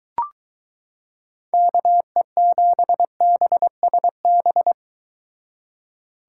Morse code, heard as a keyed tone, about 22 wpm, 700 Hz. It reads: KE7BSB